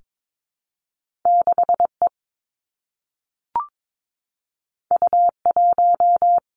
Morse code, heard as a keyed tone, about 22 wpm, 700 Hz. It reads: 6 E U 1
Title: Two Challenging Letter-Number Combinations 22wpm